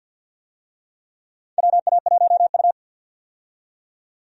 Morse code, heard as a keyed tone, about 49 wpm, 700 Hz. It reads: WA1U
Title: USA Call Signs - 4 Characters - 50wpm